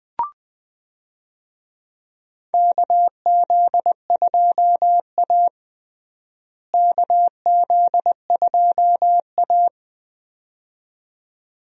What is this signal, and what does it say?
Morse code, heard as a keyed tone, about 20 wpm, 700 Hz. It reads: KZ2A KZ2A
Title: USA Call Signs - 4 Characters - Repeated 2x - 20wpm